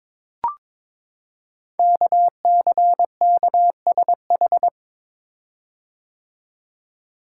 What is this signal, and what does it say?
Morse code, heard as a keyed tone, about 22 wpm, 700 Hz. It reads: KCKSH
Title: Code Groups - 5 Letters - 22wpm